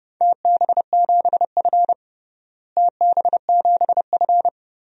Morse code, heard as a keyed tone, about 30 wpm, 700 Hz. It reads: TB7F TB7F
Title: DXCC-100 Call Signs - 4 Characters - Repeated 2x - 30wpm